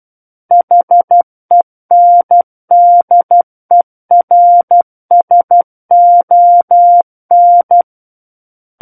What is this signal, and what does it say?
Morse code, heard as a keyed tone, about 12 wpm, 700 Hz. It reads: HENDERSON
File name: Book One / 03-Chapter 3 - 12 WPM